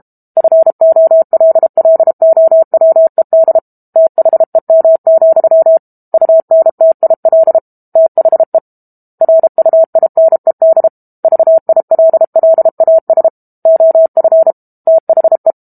Morse code, heard as a keyed tone, about 32 wpm, 650 Hz. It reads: FOLLOWED THEM, UNTIL THE RUINED VILLAS OF THE
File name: Book Two / 06-Chapter 6 - 32 WPM